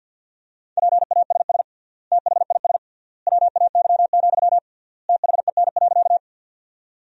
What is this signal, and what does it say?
Morse code, heard as a keyed tone, about 50 wpm, 700 Hz. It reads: PASS THIS WAY, THEN.
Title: 3x Word-Spacing - Sentences from Top 400 Words - 50wpm